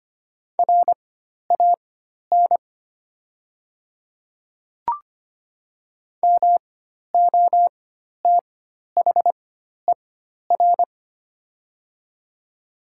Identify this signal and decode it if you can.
Morse code, heard as a keyed tone, about 25 wpm, spaced as for 12 wpm, 690 Hz. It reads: RAN MOTHER
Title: 4x Character Spacing - Top 500 Words - 25wpm